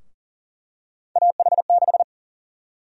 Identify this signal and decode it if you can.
Morse code, heard as a keyed tone, about 40 wpm, 700 Hz. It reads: AH6
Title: USA Call Sign Prefixes - 40wpm